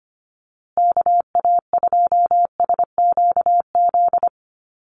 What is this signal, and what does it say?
Morse code, heard as a keyed tone, about 25 wpm, 700 Hz. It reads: KA2SQZ
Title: USA Call Signs Repeated 2x 25wpm